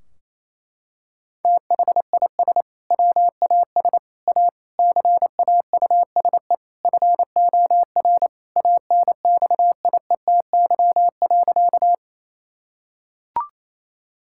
Morse code, heard as a keyed tone, about 28 wpm, 700 Hz. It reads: THIS WAS A CAUSE FOR ANXIETY.
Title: Sentences from Top 1600 Words 28wpm